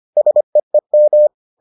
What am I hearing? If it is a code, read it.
Morse code, heard as a keyed tone, about 25 wpm, 600 Hz. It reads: SEEM